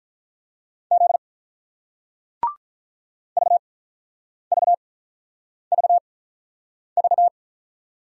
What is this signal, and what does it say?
Morse code, heard as a keyed tone, about 48 wpm, 700 Hz. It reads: Z V V V V